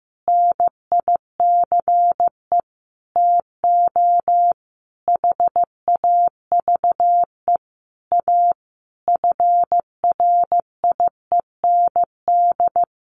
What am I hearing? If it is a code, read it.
Morse code, heard as a keyed tone, about 15 wpm, 700 Hz. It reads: NICE TO HAVE A FRIEND
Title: Sentences from Top 1100 Words 15wpm